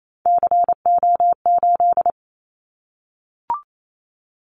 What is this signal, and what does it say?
Morse code, heard as a keyed tone, about 28 wpm, 700 Hz. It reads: CO8
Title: Callsign Prefixes for Top 100 DXCC - 28wpm